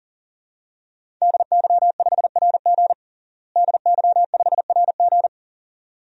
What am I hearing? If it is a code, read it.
Morse code, heard as a keyed tone, about 37 wpm, 700 Hz. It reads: DY5RG DY5RG
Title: DXCC-100 Call Signs - 5 Characters - Repeated 2x - 40wpm